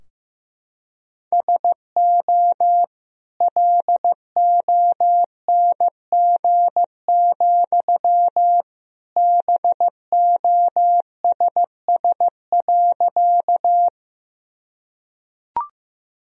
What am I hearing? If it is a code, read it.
Morse code, heard as a keyed tone, about 15 wpm, 700 Hz. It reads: SO LONG, BOSS.
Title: Sentences from Top 1000 Words - Review Part 4 - 15wpm